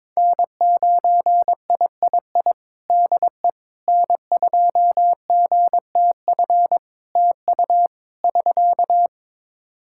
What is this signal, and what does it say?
Morse code, heard as a keyed tone, about 21 wpm, 700 Hz. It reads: N9III DE N2GTF TU <SK>
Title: Practice QSO - Exchange 4 - Ending - 22wpm